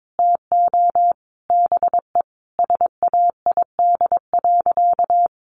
Morse code, heard as a keyed tone, about 22 wpm, 700 Hz. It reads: TO BE SAID.
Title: Sentences from Top 1000 Words 22wpm